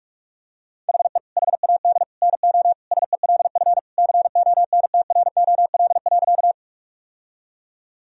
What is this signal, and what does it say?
Morse code, heard as a keyed tone, about 46 wpm, 700 Hz. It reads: HE HAD NO SELF CONTROL.
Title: Sentences from Top 1000 Words - Review Part 3 - 45wpm